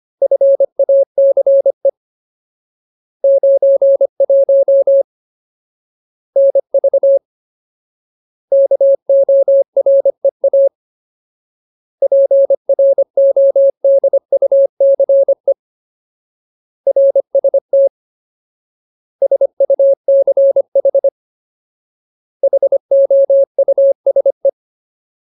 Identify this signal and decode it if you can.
Morse code, heard as a keyed tone, about 25 wpm, 550 Hz. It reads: FACE 91 NV KOREA PRODUCE RST SUCH HOUSE
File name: Ultimate Word List / 25WPM Ultimate